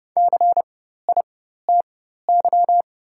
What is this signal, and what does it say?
Morse code, heard as a keyed tone, about 30 wpm, spaced as for 14 wpm, 690 Hz. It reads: CITY
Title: 4x Character Spacing - Top 500 Words - 30wpm